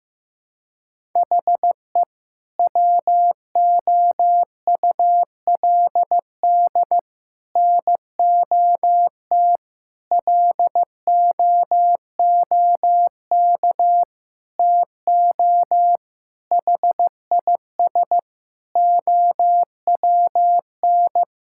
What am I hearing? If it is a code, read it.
Morse code, heard as a keyed tone, about 15 wpm, 700 Hz. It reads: HE WOULD NOT LOOK TO HIS OWN
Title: Sentences from Top 1000 Words - Review Part 6 - 15wpm